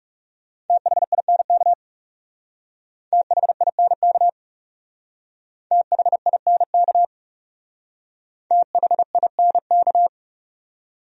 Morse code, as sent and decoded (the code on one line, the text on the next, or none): - .... .. -. -.- / - .... .. -. -.- / - .... .. -. -.- / - .... .. -. -.-
THINK THINK THINK THINK